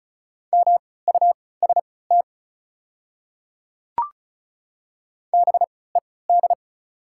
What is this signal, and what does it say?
Morse code, heard as a keyed tone, about 36 wpm, spaced as for 20 wpm, 690 Hz. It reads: MUST BED
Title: 3x Character Spacing - Top 500 Words - 35wpm